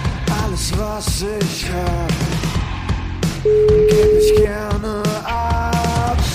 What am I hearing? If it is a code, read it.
Ringback tone (Europe, 425 Hz), with noise.